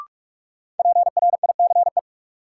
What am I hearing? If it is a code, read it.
Morse code, heard as a keyed tone, about 42 wpm, 700 Hz. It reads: PRICE
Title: Sentences from Top 600 Words - New Word Drill - 45wpm